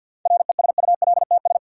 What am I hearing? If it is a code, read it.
Morse code, heard as a keyed tone, about 48 wpm, 700 Hz. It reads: RESULTS
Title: Sets of 4 Words - New Word Drill - 50wpm